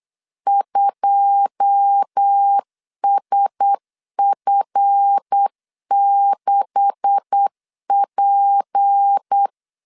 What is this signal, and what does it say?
Morse code, heard as a keyed tone, about 8 wpm, 800 Hz. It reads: 2SF6P
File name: LEZIONI / 10 SIGNS / 10D COMA 8wpm 3749